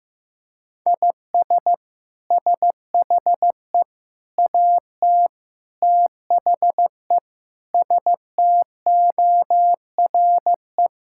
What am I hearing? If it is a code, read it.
Morse code, heard as a keyed tone, about 15 wpm, 700 Hz. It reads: IS SHE AT THE STORE
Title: Sentences from Top 1000 Words - Review Part 5 - 15wpm